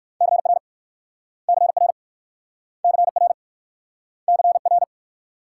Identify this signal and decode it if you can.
Morse code, heard as a keyed tone, about 59 wpm, 700 Hz. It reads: CR CR CR CR